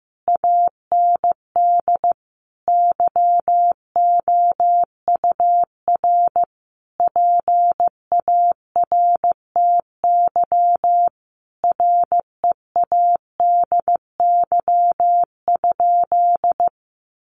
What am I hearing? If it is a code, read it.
Morse code, heard as a keyed tone, about 15 wpm, 700 Hz. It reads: AND YOUR PARTY READY?
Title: Sentences from Top 1400 Words 15wpm